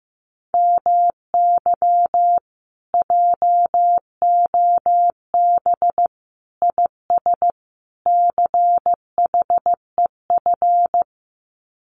Morse code, heard as a keyed tone, about 15 wpm, 700 Hz. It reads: MY JOB IS CHEF
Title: Common QSO Phrases Encore 15wpm